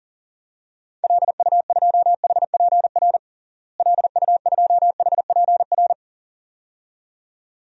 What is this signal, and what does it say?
Morse code, heard as a keyed tone, about 40 wpm, 700 Hz. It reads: LU2HPR LU2HPR
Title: DXCC-100 Call Signs - 6 Characters - Repeated 2x - 40wpm